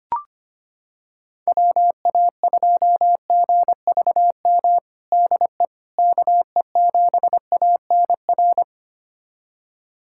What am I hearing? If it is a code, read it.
Morse code, heard as a keyed tone, about 25 wpm, 700 Hz. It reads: WA2GVM DE KE7ANR
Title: USA Callsign DE Callsign - 6 Characters - 25wpm